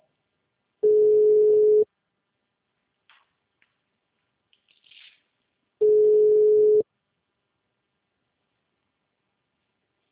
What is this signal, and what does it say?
Ringback tone (Europe, 425 Hz).